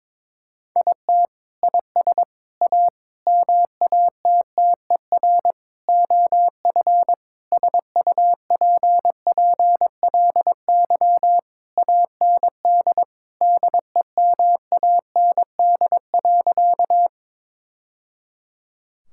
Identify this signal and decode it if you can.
Morse code, heard as a keyed tone, about 22 wpm, 700 Hz. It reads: IT IS A MATTER OF SUPPLY AND DEMAND.
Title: Sentences from Top 1000 Words - Review Part 5 - 22wpm